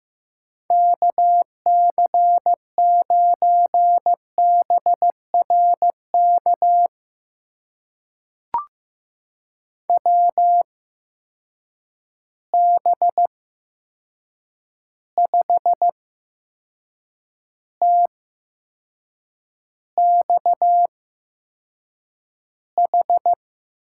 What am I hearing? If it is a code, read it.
Morse code, heard as a keyed tone, about 15 wpm, spaced as for 6 wpm, 700 Hz. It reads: KC9BRK WB5TXH
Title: USA Call Signs - 6 Characters - 8x Character Spacing - 15wpm